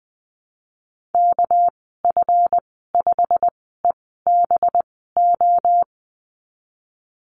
Morse code, heard as a keyed tone, about 20 wpm, spaced as for 15 wpm, 700 Hz. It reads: KF5EBO
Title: 238 Lesson 38g - Letter q - Call Signs 20wpm